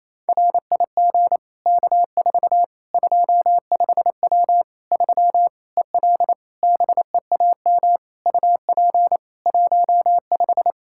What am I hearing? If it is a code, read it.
Morse code, heard as a keyed tone, about 28 wpm, 700 Hz. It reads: RIG K4 25W 3 EL BEAM UP 15
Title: Practice QSO - Exchange 2 - Info - 28wpm